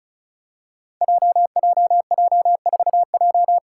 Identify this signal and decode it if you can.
Morse code, heard as a keyed tone, about 36 wpm, 700 Hz. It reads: JJJ4J
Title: Five Challenging Letter-Number Combinations 35wpm